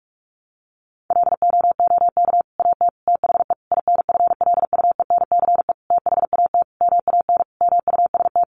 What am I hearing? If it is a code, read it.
Morse code, heard as a keyed tone, about 44 wpm, 700 Hz. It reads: LOOK AT THE INFLUENCE THAT MAN MUST
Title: Sentences from Top 1100 Words 45wpm